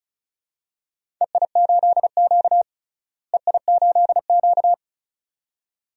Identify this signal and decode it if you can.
Morse code, heard as a keyed tone, about 36 wpm, 700 Hz. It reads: EI8Q EI8Q
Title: DXCC-100 Call Signs - 4 Characters - Repeated 2x - 35wpm